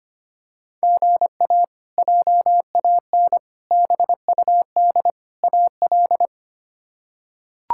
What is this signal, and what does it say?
Morse code, heard as a keyed tone, about 25 wpm, 700 Hz. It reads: GA JAN BUD AL
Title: K1USN SST QSO Practice - 25wpm